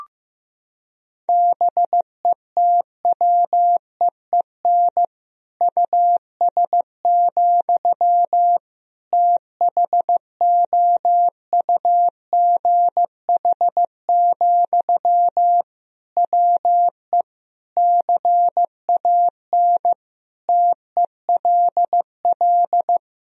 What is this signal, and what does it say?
Morse code, heard as a keyed tone, about 15 wpm, 700 Hz. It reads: BETWEEN US, THOUGH, WE CAN TELL
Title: Sentences from Top 1000 Words - Review Part 2 - 15wpm